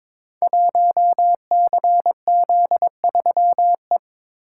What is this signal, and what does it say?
Morse code, heard as a keyed tone, about 22 wpm, 700 Hz. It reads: 1CZ3E
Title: Code Groups - 5 Letter:Numbers - 22wpm